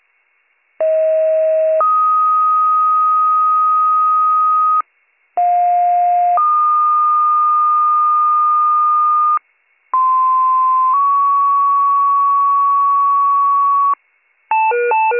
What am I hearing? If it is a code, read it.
Two-tone page, heard as a one-second tone, then a three-second tone, about 640 then 1250 Hz, about 710 then 1220 Hz, about 1040 then 1120 Hz. It is heard from 0.8 to 13.9 s.